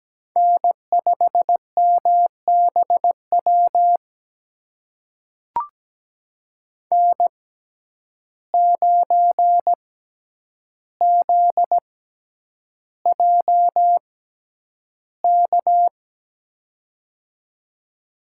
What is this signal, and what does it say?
Morse code, heard as a keyed tone, about 17 wpm, spaced as for 9 wpm, 700 Hz. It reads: N5MBW N9ZJK
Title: USA Call Signs - 5 Characters - 6x Character Spacing - 17wpm